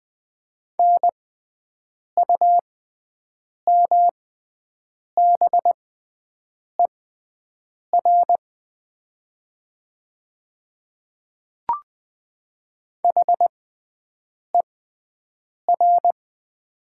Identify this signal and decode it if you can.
Morse code, heard as a keyed tone, about 20 wpm, spaced as for 7 wpm, 700 Hz. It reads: NUMBER HER
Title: 6x Character Spacing - Top 100 Words - 20wpm